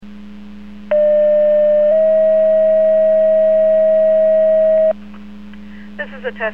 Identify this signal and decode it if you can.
Two-tone page, heard as a one-second tone, then a three-second tone, about 600 then 640 Hz, with speech. It is heard from 0.9 to 4.9 s.